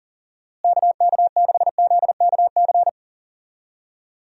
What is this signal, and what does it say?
Morse code, heard as a keyed tone, about 40 wpm, 700 Hz. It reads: KK6ZKC
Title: USA Call Signs - 6 Characters - Repeated 3x - 40wpm